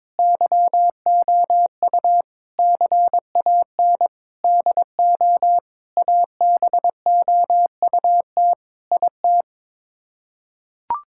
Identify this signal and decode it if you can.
Morse code, heard as a keyed tone, about 22 wpm, 700 Hz. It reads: YOU CAN DO ABOUT IT
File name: practice / 22wpm / Sets of 7 Words - 22wpm